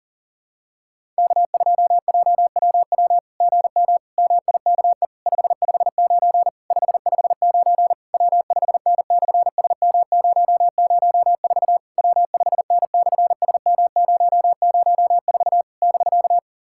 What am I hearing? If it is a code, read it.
Morse code, heard as a keyed tone, about 37 wpm, 700 Hz. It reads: K2JWW GM MIKE 559 559 W5N/SM004 W5N/SM004 <BK>
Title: SOTA QSO Practice 40wpm